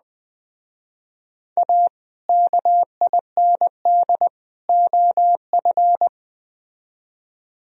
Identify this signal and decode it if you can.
Morse code, heard as a keyed tone, about 21 wpm, 700 Hz. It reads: A KIND OF